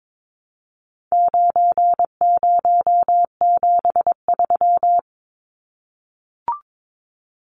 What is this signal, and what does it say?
Morse code, heard as a keyed tone, about 22 wpm, 700 Hz. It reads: 9073
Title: Numbers - 4-Digit 22wpm